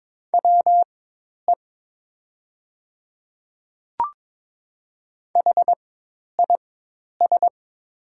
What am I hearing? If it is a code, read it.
Morse code, heard as a keyed tone, about 22 wpm, spaced as for 10 wpm, 700 Hz. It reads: WE HIS
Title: 4x Character Spacing - Top 200 Words - 22wpm